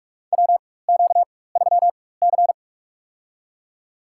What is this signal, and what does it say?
Morse code, heard as a keyed tone, about 46 wpm, spaced as for 21 wpm, 700 Hz. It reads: WQ3C
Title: USA Call Signs - 4 Characters - 4x Character Spacing - 45wpm